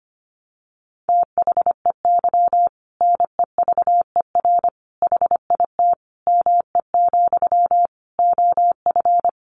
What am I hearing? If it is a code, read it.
Morse code, heard as a keyed tone, about 25 wpm, 700 Hz. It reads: THEY NEVER HIT ME, OF
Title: Sentences from Top 1200 Words 25wpm